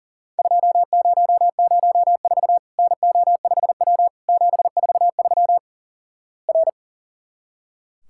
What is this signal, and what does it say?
Morse code, heard as a keyed tone, about 38 wpm, 700 Hz. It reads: 2004 NO5W 743 R
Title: Worked All Europe DX Contest - QTC Exchange - 40wpm